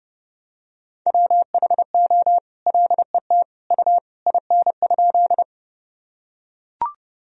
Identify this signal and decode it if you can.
Morse code, heard as a keyed tone, about 32 wpm, 700 Hz. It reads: WHO LET U IN?
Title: Sentences from Top 300 Words and Abbreviations 30wpm